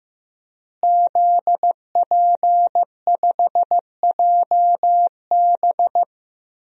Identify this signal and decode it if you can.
Morse code, heard as a keyed tone, about 16 wpm, 700 Hz. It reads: ZP5JB
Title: DXCC-100 Call Signs - 5 Characters - Repeated 2x - 15wpm